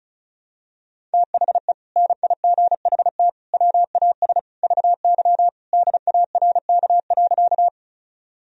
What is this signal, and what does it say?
Morse code, heard as a keyed tone, about 36 wpm, 700 Hz. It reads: THE NIGHT WAS VY DARK.
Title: Sentences from Top 500 Words and Abbreviations 35wpm